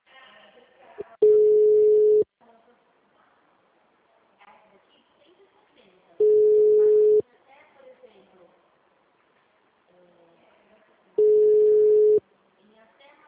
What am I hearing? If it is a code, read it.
Ringback tone (Europe, 425 Hz), with speech.
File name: out- -2014- - - .10636